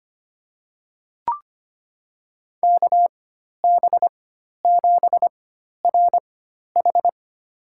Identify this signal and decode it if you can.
Morse code, heard as a keyed tone, about 25 wpm, spaced as for 12 wpm, 700 Hz. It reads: KB7RH
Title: USA Call Signs - 4x Character Spacing - 25wpm